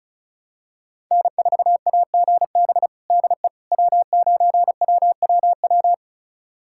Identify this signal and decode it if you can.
Morse code, heard as a keyed tone, about 36 wpm, 700 Hz. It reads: N4AGB DE W9WWW
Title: USA Callsign DE Callsign - 5 Characters - 35wpm